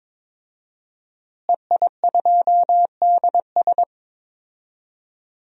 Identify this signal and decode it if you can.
Morse code, heard as a keyed tone, about 21 wpm, 700 Hz. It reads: EI2DS
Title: DXCC-100 Call Signs - 5 Characters - Repeated 3x - 22wpm